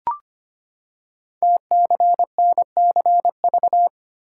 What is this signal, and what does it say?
Morse code, heard as a keyed tone, about 25 wpm, 700 Hz. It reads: TCNCV
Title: Code Groups - 5 Letters - 25wpm